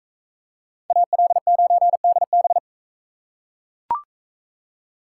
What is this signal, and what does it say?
Morse code, heard as a keyed tone, about 40 wpm, 700 Hz. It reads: AL9DB